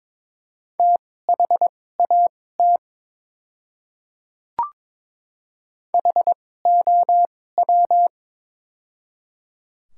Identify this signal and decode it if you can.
Morse code, heard as a keyed tone, about 22 wpm, spaced as for 16 wpm, 700 Hz. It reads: THAT HOW
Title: 2x Character Spacing - Top 100 Words - 22wpm